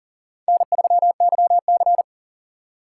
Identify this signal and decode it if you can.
Morse code, heard as a keyed tone, about 40 wpm, 700 Hz. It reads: N3YC
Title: USA Call Signs - 4 Characters - 2x Character Spacing - 40wpm